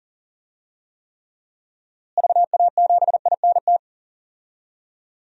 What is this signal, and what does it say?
Morse code, heard as a keyed tone, about 40 wpm, 700 Hz. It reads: VA7INT
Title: DXCC-100 Call Signs - 6 Characters - Repeated 3x - 40wpm